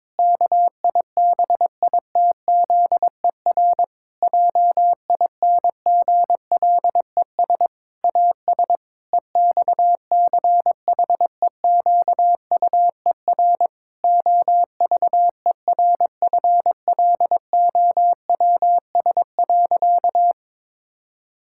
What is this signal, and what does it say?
Morse code, heard as a keyed tone, about 22 wpm, 700 Hz. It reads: KIBITZER JINGLES AS EXCHEQUER OVERFLOWS.
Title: Pangram Challenge v2 - 22wpm